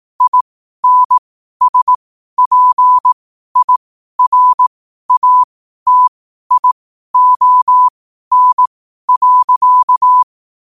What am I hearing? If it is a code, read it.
Morse code, heard as a keyed tone, about 18 wpm, spaced as for 12 wpm, 1000 Hz. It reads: INSPIRATION.
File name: qotdfiles / 12wpm